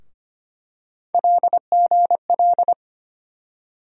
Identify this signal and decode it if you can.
Morse code, heard as a keyed tone, about 25 wpm, 700 Hz. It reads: LGL
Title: USA Call Sign Postfixes - 3 Letters 25wpm